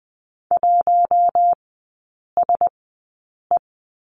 Morse code, heard as a keyed tone, about 20 wpm, spaced as for 8 wpm, 700 Hz. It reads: 1SE
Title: 041 Lesson 8e - Number 1 - Three Characters 20wpm